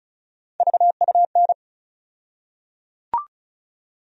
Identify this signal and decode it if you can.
Morse code, heard as a keyed tone, about 34 wpm, 700 Hz. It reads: VUN
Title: USA Call Sign Postfixes - 3 Letters 35wpm